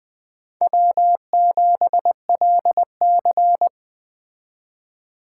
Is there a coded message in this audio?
Yes: Morse code from 0.6 to 3.7 s, heard as a keyed tone, about 20 wpm, 700 Hz. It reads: W7LC